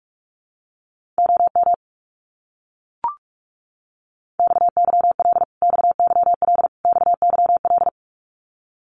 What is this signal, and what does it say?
Morse code, heard as a keyed tone, about 44 wpm, 700 Hz. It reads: OM XYL XYL XYL
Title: Abbreviations Repeated 3x - 45wpm